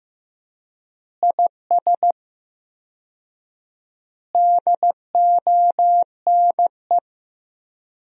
Morse code, heard as a keyed tone, about 15 wpm, 700 Hz. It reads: IS DONE
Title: 4x Word-Spacing - Sentences from Top 500 Words - 15wpm